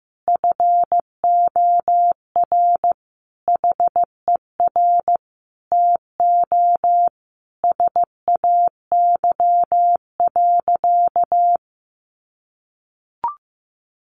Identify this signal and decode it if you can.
Morse code, heard as a keyed tone, about 15 wpm, 700 Hz. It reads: FOR HER TO SAY.
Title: Sentences from Top 1400 Words 15wpm